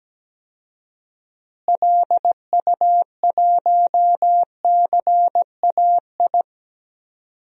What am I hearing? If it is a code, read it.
Morse code, heard as a keyed tone, about 17 wpm, 700 Hz. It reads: LU1CAI
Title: DXCC-100 Call Signs - 6 Characters - Repeated 3x - 17wpm